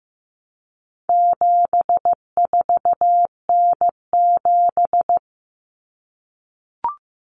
Morse code, heard as a keyed tone, about 15 wpm, 700 Hz. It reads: 74N7
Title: Common Cut Numbers - 4-Digit 15wpm